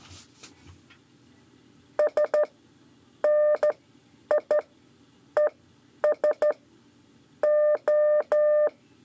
Morse code, heard as a keyed tone, about 12 wpm, spaced as for 9 wpm, 600 Hz. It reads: S NI ES O
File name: 5 LEZIONE RIPASSO 2